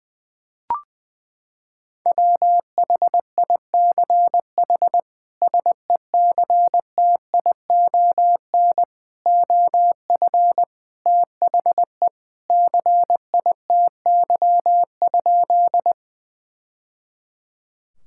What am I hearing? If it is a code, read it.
Morse code, heard as a keyed tone, about 20 wpm, 700 Hz. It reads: WHICH SECTION OF THE CITY?
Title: Sentences from Top 1000 Words - Review Part 5 - 20wpm